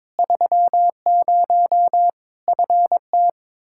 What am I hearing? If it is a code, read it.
Morse code, heard as a keyed tone, about 22 wpm, 700 Hz. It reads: 30 FT
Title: Practice QSO - Exchange 2 - Info - 22wpm